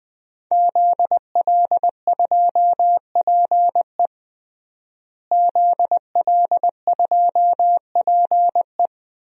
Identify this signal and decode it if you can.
Morse code, heard as a keyed tone, about 21 wpm, 700 Hz. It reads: ZL2PE ZL2PE
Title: DXCC-100 Call Signs - 5 Characters - Repeated 3x - 20wpm